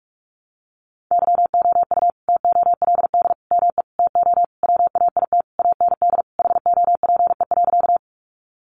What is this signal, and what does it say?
Morse code, heard as a keyed tone, about 44 wpm, 700 Hz. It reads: YOU TOLD ME TO WAIT AND HOPE.
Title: Sentences from Top 1000 Words 45wpm